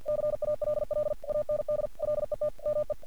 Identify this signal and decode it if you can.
Morse code, heard as a keyed tone, about 49 wpm, 600 Hz. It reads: CALL AND LET ME
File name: archived / CRSnet